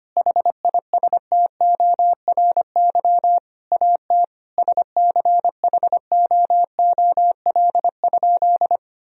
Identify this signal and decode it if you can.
Morse code, heard as a keyed tone, about 25 wpm, 700 Hz. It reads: HISTORY AT SCHOOL?
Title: Sentences from Top 1000 Words - Review Part 4 - 25wpm